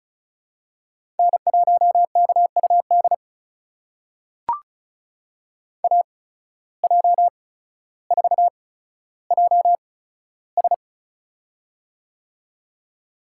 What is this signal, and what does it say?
Morse code, heard as a keyed tone, about 34 wpm, spaced as for 15 wpm, 700 Hz. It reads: N1KUD AJ4JS
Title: USA Call Signs - 5 Characters - 8x Character Spacing - 35wpm